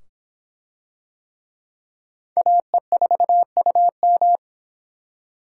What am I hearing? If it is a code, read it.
Morse code, heard as a keyed tone, about 25 wpm, 700 Hz. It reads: AE4UM